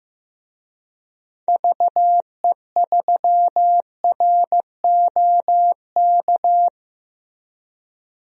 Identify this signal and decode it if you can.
Morse code, heard as a keyed tone, about 15 wpm, 700 Hz. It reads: VE3ROK
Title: DXCC-100 Call Signs - 6 Characters - Repeated 2x - 15wpm